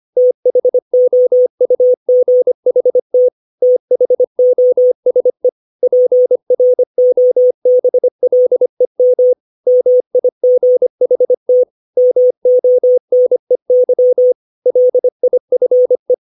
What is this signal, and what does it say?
Morse code, heard as a keyed tone, about 25 wpm, 500 Hz. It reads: THOUGHT THOSE PROBLEM MIGHT MONEY LIFE
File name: 632 words / 632 Words 25WPM